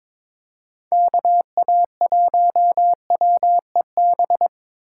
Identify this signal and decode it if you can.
Morse code, heard as a keyed tone, about 22 wpm, 700 Hz. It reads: KA1WEB
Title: USA Call Signs - 6 Characters - 4x Character Spacing - 22wpm